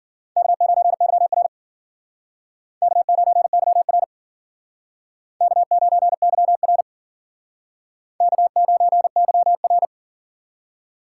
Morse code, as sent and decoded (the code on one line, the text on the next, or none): -.- ----. -.-- .-. / -.- ----. -.-- .-. / -.- ----. -.-- .-. / -.- ----. -.-- .-.
K9YR K9YR K9YR K9YR